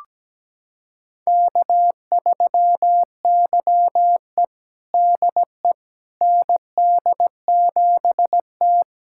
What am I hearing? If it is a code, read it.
Morse code, heard as a keyed tone, about 17 wpm, 700 Hz. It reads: K3YE DE ND7T
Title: USA Callsign DE Callsign - 4 Characters - 17wpm